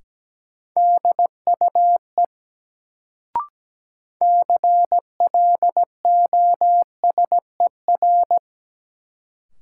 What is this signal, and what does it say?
Morse code, heard as a keyed tone, about 17 wpm, 700 Hz. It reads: DUE CLOSER
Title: Sets of 5 Words Encore - New Word Drill - 17wpm